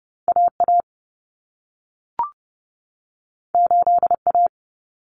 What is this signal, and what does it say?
Morse code, heard as a keyed tone, about 30 wpm, 700 Hz. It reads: AA 8A
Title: Common Cut Numbers - 2-Digit 30wpm